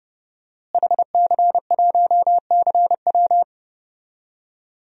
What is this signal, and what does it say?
Morse code, heard as a keyed tone, about 30 wpm, 700 Hz. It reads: HC1CW
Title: DXCC-100 Call Signs - 5 Characters - Repeated 3x - 30wpm